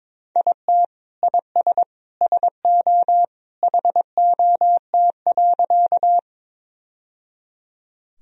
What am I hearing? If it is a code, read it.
Morse code, heard as a keyed tone, about 22 wpm, 700 Hz. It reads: IT IS SO HOT.
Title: Sentences from Top 1000 Words - Review Part 3 - 22wpm